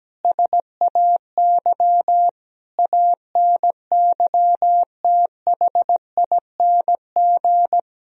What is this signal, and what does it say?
Morse code, heard as a keyed tone, about 17 wpm, 700 Hz. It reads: SAY ANYTHING
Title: Sentences from Top 1000 Words - Review Part 2 - 17wpm